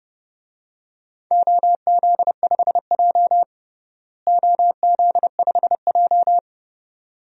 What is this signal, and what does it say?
Morse code, heard as a keyed tone, about 32 wpm, 700 Hz. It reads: OZ5J OZ5J
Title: DXCC-100 Call Signs - 4 Characters - Repeated 2x - 30wpm